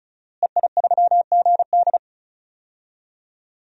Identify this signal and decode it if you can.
Morse code, heard as a keyed tone, about 36 wpm, 700 Hz. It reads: EI3GD
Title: DXCC-100 Call Signs - 5 Characters - Repeated 2x - 35wpm